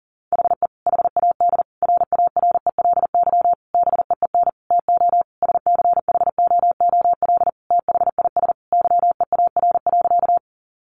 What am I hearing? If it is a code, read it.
Morse code, heard as a keyed tone, about 40 wpm, 700 Hz. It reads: HE HAD RARELY BEEN TO SCHOOL THIS YEAR.
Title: Sentences from Top 1400 Words 40wpm